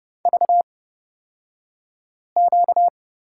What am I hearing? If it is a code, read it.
Morse code, heard as a keyed tone, about 30 wpm, spaced as for 5 wpm, 700 Hz. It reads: VQ
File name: course / 30:10wpm / 235 Lesson 38d - Letter q - Two Characters 30wpm